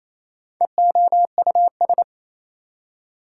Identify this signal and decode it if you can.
Morse code, heard as a keyed tone, about 27 wpm, 700 Hz. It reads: EOUS
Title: Prefixes and Suffixes 28wpm